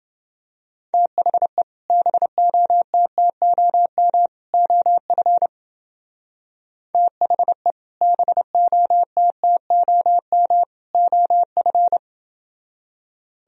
Morse code, heard as a keyed tone, about 28 wpm, 700 Hz. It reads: THE BOTTOM OF THE BOTTOM OF